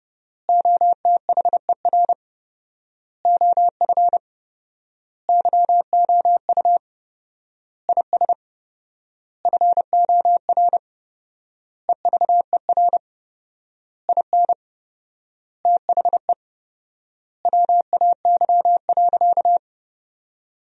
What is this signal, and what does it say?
Morse code, heard as a keyed tone, about 30 wpm, 700 Hz. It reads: OTHER OF YOU IS FOR EVER IN THE WAY.
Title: 4x Word-Spacing - Sentences from Top 400 Words - 30wpm